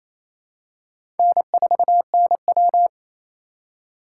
Morse code, heard as a keyed tone, about 28 wpm, 700 Hz. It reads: N4NW
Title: K1USN SST QSO Practice - 28wpm